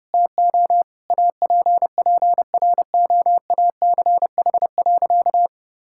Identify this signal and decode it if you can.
Morse code, heard as a keyed tone, about 30 wpm, 700 Hz. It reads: TO APPROACH.
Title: Sentences from Top 1000 Words - Review Part 6 - 30wpm